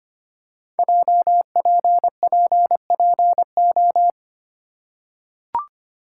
Morse code, heard as a keyed tone, about 25 wpm, 700 Hz. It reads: JPPPO
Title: Code Groups - 5 Letters - 25wpm